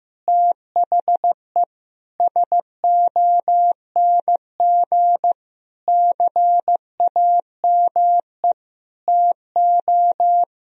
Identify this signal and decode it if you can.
Morse code, heard as a keyed tone, about 15 wpm, 700 Hz. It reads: THE SONG CAME TO
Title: Sentences from Top 1000 Words - Review Part 2 - 15wpm